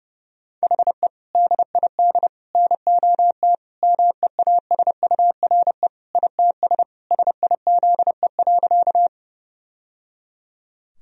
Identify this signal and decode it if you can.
Morse code, heard as a keyed tone, about 30 wpm, 700 Hz. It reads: HE DID NOT MEASURE ITS SIZE.
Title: Sentences from Top 1000 Words - Review Part 2 - 30wpm